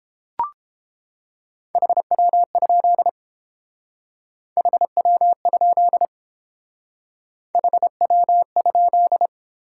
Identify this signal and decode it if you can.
Morse code, heard as a keyed tone, about 29 wpm, 690 Hz. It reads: HW? HW? HW?